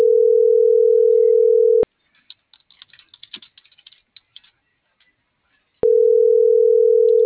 Ringback tone (North America), 440 + 480 Hz.